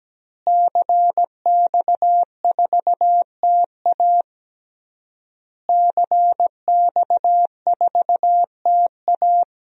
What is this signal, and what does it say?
Morse code, heard as a keyed tone, about 17 wpm, 700 Hz. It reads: CX4TA CX4TA
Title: DXCC-100 Call Signs - 5 Characters - Repeated 3x - 17wpm